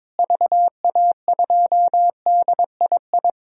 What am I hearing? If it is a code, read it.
Morse code, heard as a keyed tone, about 23 wpm, 700 Hz. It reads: VA2DII
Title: DXCC-100 Call Signs - 6 Characters - Repeated 2x - 22wpm